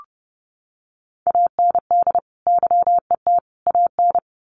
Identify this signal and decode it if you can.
Morse code, heard as a keyed tone, about 30 wpm, 700 Hz. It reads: AND YET AN
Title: Sentences from Top 1300 Words 30wpm